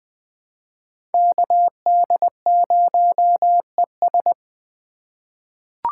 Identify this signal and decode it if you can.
Morse code, heard as a keyed tone, about 20 wpm, 700 Hz. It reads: KD0ES
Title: USA Call Signs - 4x Character Spacing - 20wpm